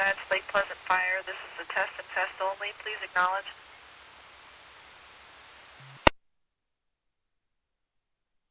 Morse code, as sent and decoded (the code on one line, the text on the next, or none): none